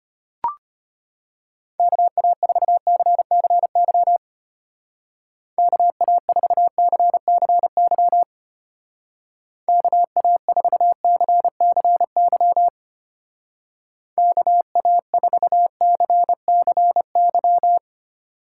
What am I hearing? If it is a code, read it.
Morse code, heard as a keyed tone, about 30 wpm, 700 Hz. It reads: KA4CCY KA4CCY KA4CCY KA4CCY
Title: Speed Racing - USA 2x3 Call Signs 25wpm